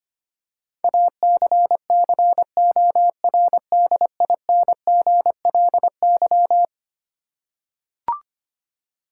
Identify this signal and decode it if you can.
Morse code, heard as a keyed tone, about 25 wpm, 700 Hz. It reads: ACCORDINGLY
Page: morse-code-ninja-practice-25wpm